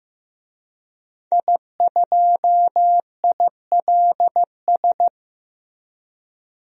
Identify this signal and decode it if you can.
Morse code, heard as a keyed tone, about 15 wpm, 700 Hz. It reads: I2ILS
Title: DXCC-100 Call Signs - 5 Characters - 15wpm